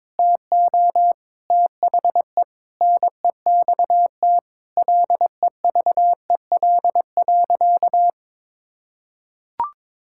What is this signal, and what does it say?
Morse code, heard as a keyed tone, about 22 wpm, 700 Hz. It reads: TO THE NEXT LEVEL.
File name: Sentences with 7-Word Phrases - 22wpm